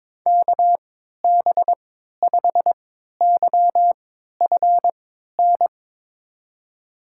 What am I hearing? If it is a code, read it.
Morse code, heard as a keyed tone, about 23 wpm, spaced as for 13 wpm, 700 Hz. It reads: KB5YFN
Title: USA Call Signs - 6 Characters - 3x Character Spacing - 22wpm